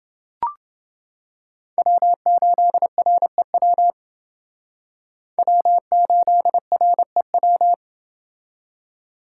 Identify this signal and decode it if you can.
Morse code, heard as a keyed tone, about 28 wpm, 700 Hz. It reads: W8REW W8REW